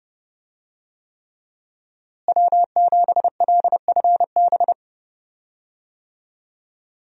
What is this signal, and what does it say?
Morse code, heard as a keyed tone, about 30 wpm, 700 Hz. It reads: W7LFB
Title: USA Call Signs - 5 Characters - Repeated 3x - 30wpm